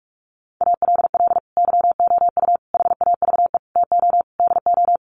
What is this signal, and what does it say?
Morse code, heard as a keyed tone, about 44 wpm, 700 Hz. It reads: ALL YOU HAVE TO DO
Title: Sets of 5 Words 45wpm